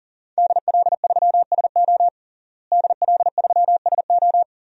Morse code, heard as a keyed tone, about 40 wpm, 700 Hz. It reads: DL3SO DL3SO
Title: DXCC-100 Call Signs - 5 Characters - Repeated 3x - 40wpm